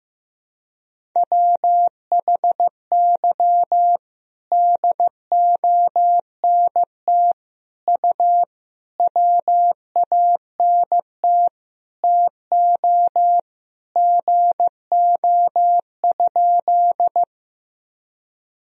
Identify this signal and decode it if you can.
Morse code, heard as a keyed tone, about 16 wpm, 700 Hz. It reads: WHY DONT U WANT TO GO?
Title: Sentences from Top 300 Words and Abbreviations 15wpm